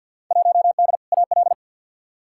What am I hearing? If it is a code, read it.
Morse code, heard as a keyed tone, about 53 wpm, 700 Hz. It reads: 1D AL
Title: Field Day Exchanges - 50wpm